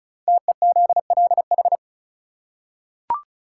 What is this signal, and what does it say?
Morse code, heard as a keyed tone, about 36 wpm, 700 Hz. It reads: TEZLH
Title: Code Groups - 5 Letters - 35wpm